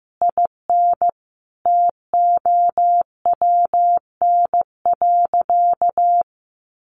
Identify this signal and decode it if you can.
Morse code, heard as a keyed tone, about 15 wpm, 700 Hz. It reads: IN TOWN.
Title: Sentences from Top 600 Words 15wpm